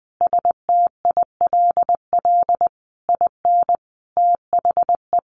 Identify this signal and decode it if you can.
Morse code, heard as a keyed tone, about 20 wpm, 700 Hz. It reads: STILL IN THE
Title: A Sound Of Thunder 20wpm